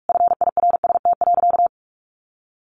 Morse code, heard as a keyed tone, about 44 wpm, 700 Hz. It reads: FIRST.
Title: Sentences from Top 1200 Words 45wpm